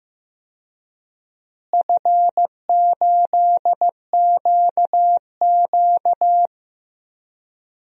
Morse code, heard as a keyed tone, about 16 wpm, 700 Hz. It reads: F8QQ
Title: DXCC-100 Call Signs - 4 Characters - Repeated 3x - 15wpm